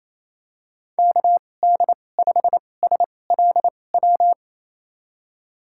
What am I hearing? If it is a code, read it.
Morse code, heard as a keyed tone, about 28 wpm, spaced as for 20 wpm, 700 Hz. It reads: KD5SLW
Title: USA Call Signs - 2x Character Spacing - 28wpm